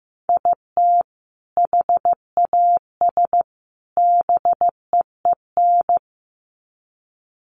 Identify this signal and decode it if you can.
Morse code, heard as a keyed tone, about 15 wpm, 700 Hz. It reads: IT HAS BEEN
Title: Sets of 3 Words 15wpm